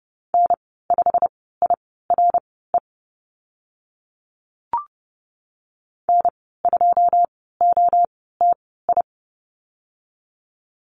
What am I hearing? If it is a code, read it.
Morse code, heard as a keyed tone, about 30 wpm, spaced as for 17 wpm, 690 Hz. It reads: N5IRE N2OTI
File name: Lesson 15g - Number 5 - Call Signs 30wpm